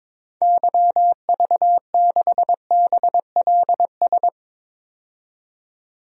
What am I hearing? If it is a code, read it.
Morse code, heard as a keyed tone, about 22 wpm, 700 Hz. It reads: YV6BLS
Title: DXCC-100 Call Signs - 6 Characters - 22wpm